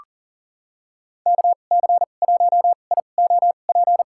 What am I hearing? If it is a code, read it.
Morse code, heard as a keyed tone, about 40 wpm, spaced as for 29 wpm, 700 Hz. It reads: KC1IOP
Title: USA Call Signs - 6 Characters - 2x Character Spacing - 40wpm